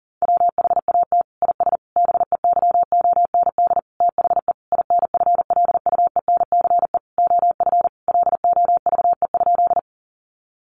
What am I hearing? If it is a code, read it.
Morse code, heard as a keyed tone, about 40 wpm, 700 Hz. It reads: WHAT IS BEYOND THE INFLUENCE OF LOVE?
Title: Sentences from Top 1100 Words 40wpm